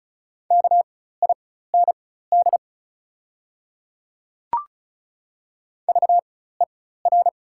Morse code, heard as a keyed tone, about 34 wpm, spaced as for 16 wpm, 690 Hz. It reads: KIND VER
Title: 4x Character Spacing - Top 200 Words - 35wpm